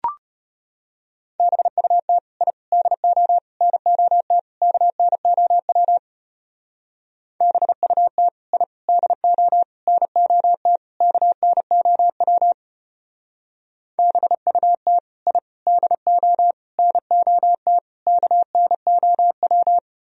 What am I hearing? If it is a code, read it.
Morse code, heard as a keyed tone, about 32 wpm, 700 Hz. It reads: BUT I DO NOT KNOW BUT I DO NOT KNOW BUT I DO NOT KNOW